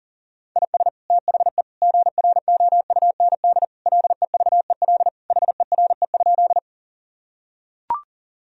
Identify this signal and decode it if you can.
Morse code, heard as a keyed tone, about 40 wpm, 700 Hz. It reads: IS THE GROUND LEVEL HERE?
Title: Sentences from Top 1000 Words - Review Part 4 - 40wpm